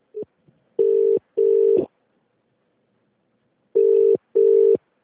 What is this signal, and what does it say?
Ringback tone (UK), 400 + 450 Hz.